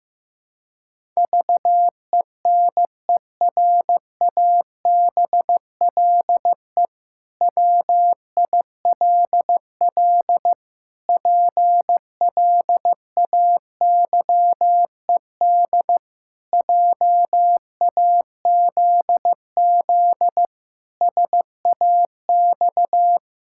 Morse code, heard as a keyed tone, about 15 wpm, 700 Hz. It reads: VENERABLE WILL PLAYED JAZZ SAX
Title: Pangram Challenge v2 - 15wpm